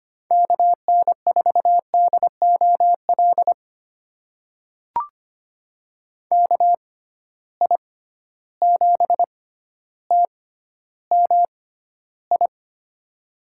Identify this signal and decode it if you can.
Morse code, heard as a keyed tone, about 25 wpm, spaced as for 13 wpm, 700 Hz. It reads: KN4DOL KI7TMI
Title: USA Call Signs - 6 Characters - 6x Character Spacing - 25wpm